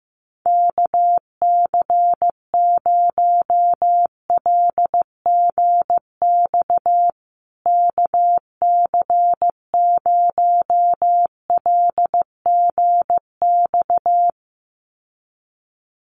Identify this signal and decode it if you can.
Morse code, heard as a keyed tone, about 15 wpm, 700 Hz. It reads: KC0LGX KC0LGX
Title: USA Call Signs Repeated 2x 15wpm